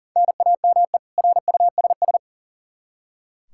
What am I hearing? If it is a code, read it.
Morse code, heard as a keyed tone, about 40 wpm, 700 Hz. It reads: NAME RUSS
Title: Practice QSO - Exchange 1 - Beginning - 40wpm